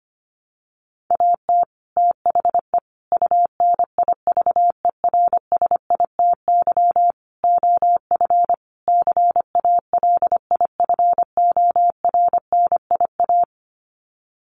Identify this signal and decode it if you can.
Morse code, heard as a keyed tone, about 25 wpm, 700 Hz. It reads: AT THE UNIVERSITY OF CALIFORNIA
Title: Sets of 5 Words Encore2 25wpm